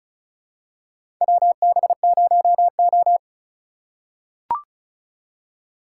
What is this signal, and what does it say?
Morse code, heard as a keyed tone, about 34 wpm, 700 Hz. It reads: WB0O
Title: USA Call Signs - 4 Characters - 4x Character Spacing - 35wpm